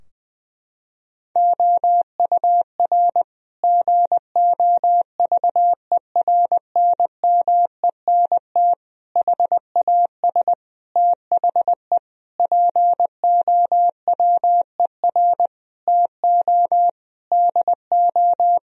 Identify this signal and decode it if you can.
Morse code, heard as a keyed tone, about 20 wpm, 700 Hz. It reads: OUR GOVERNMENT HAS THE POWER TO DO
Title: Sentences from Top 1000 Words - Review Part 3 - 20wpm